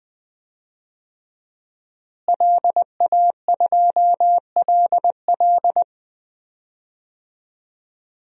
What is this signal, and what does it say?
Morse code, heard as a keyed tone, about 20 wpm, 700 Hz. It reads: LA2LL
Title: DXCC-100 Call Signs - 5 Characters - Repeated 2x - 20wpm